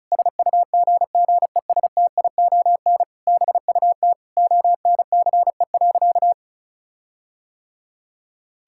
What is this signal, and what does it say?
Morse code, heard as a keyed tone, about 34 wpm, 700 Hz. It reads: SUGGESTION BUT ONCE.
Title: Sentences from Top 1600 Words 35wpm